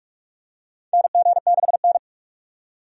Morse code, heard as a keyed tone, about 43 wpm, 700 Hz. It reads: NG6N
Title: USA Call Signs - 4 Characters - 45wpm